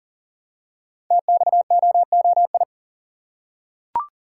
Morse code, heard as a keyed tone, about 40 wpm, 700 Hz. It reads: TXOOI
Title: Code Groups - 5 Letters - 40wpm